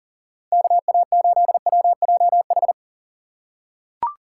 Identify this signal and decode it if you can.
Morse code, heard as a keyed tone, about 39 wpm, 700 Hz. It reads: KA8WJH
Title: USA Call Signs - 6 Characters - 3x Character Spacing - 40wpm